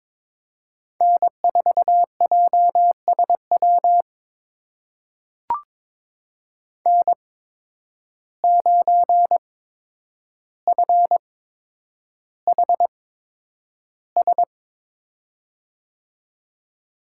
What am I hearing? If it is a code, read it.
Morse code, heard as a keyed tone, about 22 wpm, spaced as for 9 wpm, 700 Hz. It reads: N4JSW N9FHS
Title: USA Call Signs - 5 Characters - 8x Character Spacing - 22wpm